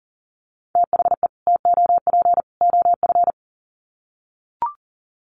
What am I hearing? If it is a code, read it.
Morse code, heard as a keyed tone, about 40 wpm, 700 Hz. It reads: THE TOP OF
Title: Sets of 3 Words 40wpm